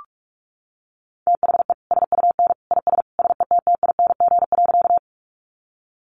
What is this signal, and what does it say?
Morse code, heard as a keyed tone, about 46 wpm, 700 Hz. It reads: THE SUN IS SETTING.
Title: Sentences from Top 1300 Words 45wpm